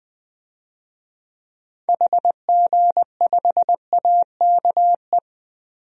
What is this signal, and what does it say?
Morse code, heard as a keyed tone, about 20 wpm, 700 Hz. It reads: HG5AKE
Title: DXCC-100 Call Signs - 6 Characters - 20wpm